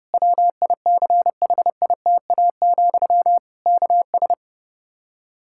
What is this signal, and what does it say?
Morse code, heard as a keyed tone, about 32 wpm, 700 Hz. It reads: WICHITA, KS
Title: K1USN SST QSO Practice 30wpm